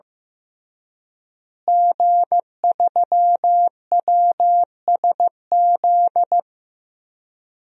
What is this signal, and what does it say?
Morse code, heard as a keyed tone, about 15 wpm, 700 Hz. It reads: G3WSZ
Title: DXCC-100 Call Signs - 5 Characters - Repeated 3x - 15wpm